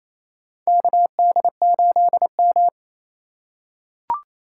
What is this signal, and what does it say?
Morse code, heard as a keyed tone, about 28 wpm, 700 Hz. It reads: KD8M
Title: USA Call Signs - 4 Characters - 6x Character Spacing - 28wpm